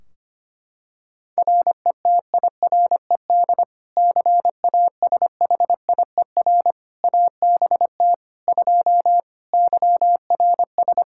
Morse code, heard as a keyed tone, about 25 wpm, 700 Hz. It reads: RETIRED CASHIER ABT 2 YRS
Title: K1USN SST QSO Practice 25wpm